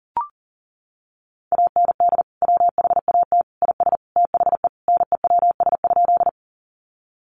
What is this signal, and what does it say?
Morse code, heard as a keyed tone, about 40 wpm, 700 Hz. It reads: AND WHAT IS THE NEWS?
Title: Sentences from Top 800 Words 40wpm